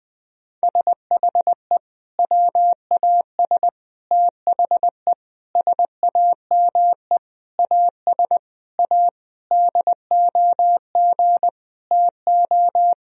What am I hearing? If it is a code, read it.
Morse code, heard as a keyed tone, about 21 wpm, 700 Hz. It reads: SHE WAS THE SAME AS A DOG TO
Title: Sentences from Top 1000 Words - Review Part 2 - 20wpm